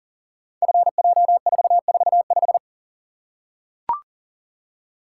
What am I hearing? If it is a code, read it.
Morse code, heard as a keyed tone, about 38 wpm, 700 Hz. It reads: FJ445
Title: Code Groups - 5 Letter:Numbers - 40wpm